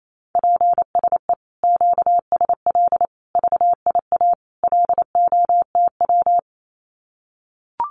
Morse code, heard as a keyed tone, about 28 wpm, 700 Hz. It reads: PSE QSL VIA LOTW
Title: Common QSO Phrases Encore 28wpm